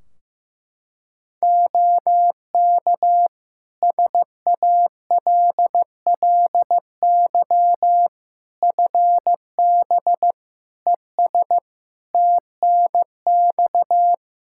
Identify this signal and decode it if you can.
Morse code, heard as a keyed tone, about 15 wpm, 700 Hz. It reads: OK SALLY FB ES TNX
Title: Practice QSO - Exchange 2 - Info - 15wpm